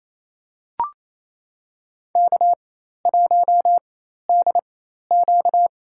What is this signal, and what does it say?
Morse code, heard as a keyed tone, about 28 wpm, spaced as for 13 wpm, 700 Hz. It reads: K1DQ
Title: USA Call Signs - 4 Characters - 4x Character Spacing - 28wpm